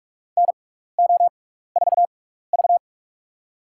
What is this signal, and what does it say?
Morse code, heard as a keyed tone, about 46 wpm, spaced as for 15 wpm, 700 Hz. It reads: NO4V
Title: USA Call Signs - 4 Characters - 6x Character Spacing - 45wpm